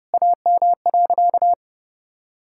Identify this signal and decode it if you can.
Morse code, heard as a keyed tone, about 32 wpm, 700 Hz. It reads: AM.
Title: Sentences from Top 1000 Words - Review Part 6 - 30wpm